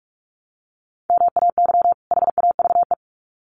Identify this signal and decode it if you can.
Morse code, heard as a keyed tone, about 44 wpm, 700 Hz. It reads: MAY HAVE
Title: Sets of 2 Words 45wpm